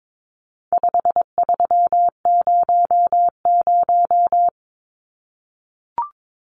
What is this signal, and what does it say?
Morse code, heard as a keyed tone, about 22 wpm, 700 Hz. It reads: 5300
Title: Numbers - 4-Digit 22wpm